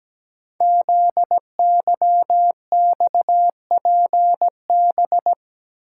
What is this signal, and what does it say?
Morse code, heard as a keyed tone, about 17 wpm, 700 Hz. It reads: ZYXPB
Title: Code Groups - 5 Letter:Numbers - 17wpm